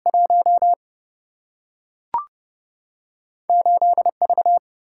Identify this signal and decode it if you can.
Morse code, heard as a keyed tone, about 31 wpm, 700 Hz. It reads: 1 8 V
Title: Single-Letter Single-Number 30wpm